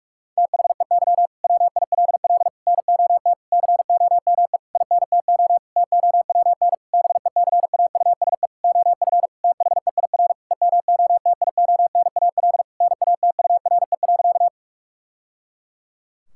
Morse code, heard as a keyed tone, about 46 wpm, 700 Hz. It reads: THEY WILL NOT COME INTO TOWN BECAUSE OF THEIR EMOTIONAL NATURE.
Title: Sentences from Top 1000 Words - Review Part 5 - 45wpm